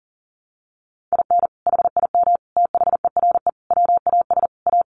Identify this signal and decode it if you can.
Morse code, heard as a keyed tone, about 40 wpm, 700 Hz. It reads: IN HIM THERE WAS A
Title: Sentences from Top 1300 Words 40wpm